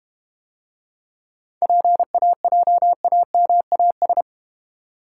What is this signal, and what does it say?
Morse code, heard as a keyed tone, about 32 wpm, 700 Hz. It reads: PAJAMAS